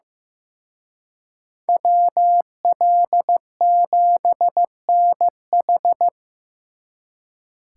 Morse code, heard as a keyed tone, about 18 wpm, 700 Hz. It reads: WL7NH
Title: Speed Racing - USA Call Signs 15wpm